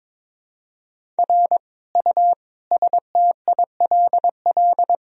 Morse code, heard as a keyed tone, about 21 wpm, 700 Hz. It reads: R U STILL
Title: Sentences from Top 400 Words and Abbreviations 22wpm